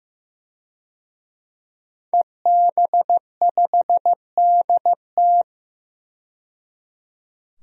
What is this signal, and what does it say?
Morse code, heard as a keyed tone, about 15 wpm, 700 Hz. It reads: EB5DT
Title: DXCC-100 Call Signs - 5 Characters - 15wpm